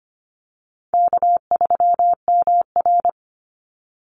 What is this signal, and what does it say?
Morse code, heard as a keyed tone, about 25 wpm, 700 Hz. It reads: K3MR
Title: Call Signs Repeated 3x - 25wpm